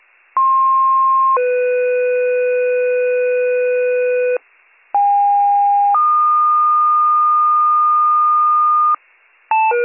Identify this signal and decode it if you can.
Two-tone page, heard as a one-second tone, then a three-second tone, about 1060 then 500 Hz, about 810 then 1220 Hz, with noise. It is heard from 0.4 to 9.0 s.